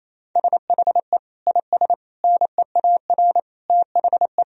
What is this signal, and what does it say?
Morse code, heard as a keyed tone, about 27 wpm, 700 Hz. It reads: SHE IS NEAR THE
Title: Sentences from Top 1000 Words - Review Part 1 - 28wpm